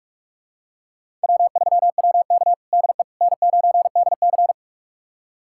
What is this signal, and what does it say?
Morse code, heard as a keyed tone, about 44 wpm, 700 Hz. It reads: W3WK DE N9DC
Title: USA Callsign DE Callsign - 4 Characters - 45wpm